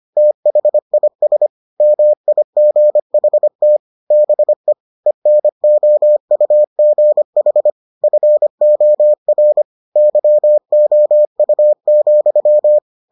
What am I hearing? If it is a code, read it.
Morse code, heard as a keyed tone, about 25 wpm, 600 Hz. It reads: THIS MIGHT BE ENOUGH FOR YOU,